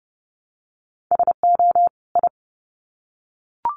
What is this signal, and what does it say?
Morse code, heard as a keyed tone, about 30 wpm, 700 Hz. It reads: SO I
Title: Sets of 2 Words 30wpm